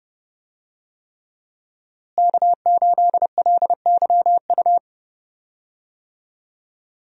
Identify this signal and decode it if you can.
Morse code, heard as a keyed tone, about 30 wpm, 700 Hz. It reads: K8LYU
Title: USA Call Signs - 5 Characters - Repeated 3x - 30wpm